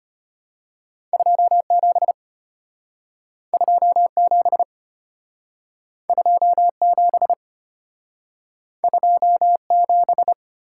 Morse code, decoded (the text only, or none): 27 27 27 27